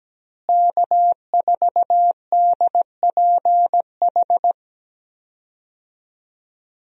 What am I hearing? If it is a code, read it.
Morse code, heard as a keyed tone, about 17 wpm, 700 Hz. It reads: K4DPH
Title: USA Call Signs - 5 Characters - Repeated 3x - 17wpm